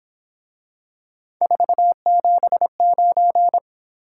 Morse code, heard as a keyed tone, about 26 wpm, 700 Hz. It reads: 479